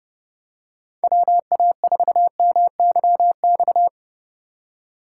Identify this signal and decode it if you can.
Morse code, heard as a keyed tone, about 30 wpm, 700 Hz. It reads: WA4MYX
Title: USA Call Signs - 6 Characters - 3x Character Spacing - 30wpm